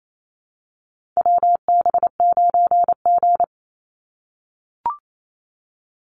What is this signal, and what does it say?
Morse code, heard as a keyed tone, about 28 wpm, 700 Hz. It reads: WB9G
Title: Call Signs - 28wpm